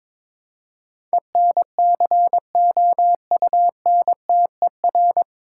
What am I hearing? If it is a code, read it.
Morse code, heard as a keyed tone, about 22 wpm, 700 Hz. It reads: ENCOUNTER
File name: 9 Letter Words - First 3 Letters Given - 22wpm